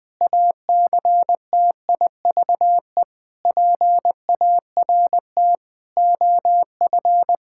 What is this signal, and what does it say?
Morse code, heard as a keyed tone, about 20 wpm, 700 Hz. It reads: ACTIVE PART OF
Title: Sentences from Top 1000 Words - Review Part 6 - 20wpm